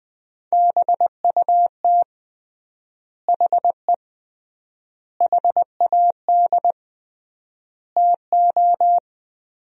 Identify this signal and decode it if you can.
Morse code, heard as a keyed tone, about 20 wpm, 700 Hz. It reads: BUT HE HAD TO
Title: 3x Word-Spacing - Sentences from Top 100 Words - 20wpm